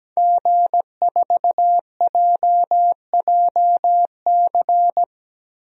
Morse code, heard as a keyed tone, about 17 wpm, 700 Hz. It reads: G4JJC
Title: DXCC-100 Call Signs - 5 Characters - Repeated 2x - 17wpm